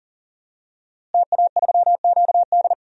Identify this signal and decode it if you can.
Morse code, heard as a keyed tone, about 40 wpm, 700 Hz. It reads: TA3QD
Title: DXCC-100 Call Signs - 5 Characters - Repeated 3x - 40wpm